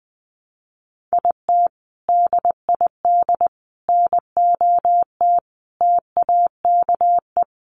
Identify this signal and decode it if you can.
Morse code, heard as a keyed tone, about 20 wpm, 700 Hz. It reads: IT DID NOT TAKE
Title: Sentences from Top 600 Words 20wpm